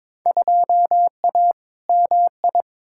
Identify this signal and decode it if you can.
Morse code, heard as a keyed tone, about 22 wpm, 700 Hz. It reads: 2A MI
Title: Field Day Exchanges Encore - 22wpm